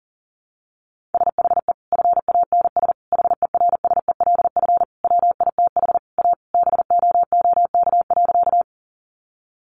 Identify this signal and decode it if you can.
Morse code, heard as a keyed tone, about 40 wpm, 700 Hz. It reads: SHE FANS HERSELF WITH A BOOK.
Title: Easy Phrases 40wpm